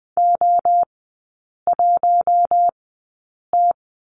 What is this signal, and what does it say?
Morse code, heard as a keyed tone, about 19 wpm, spaced as for 8 wpm, 700 Hz. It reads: O1T
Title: 067 Lesson 12e - Letter d - Three Characters 20wpm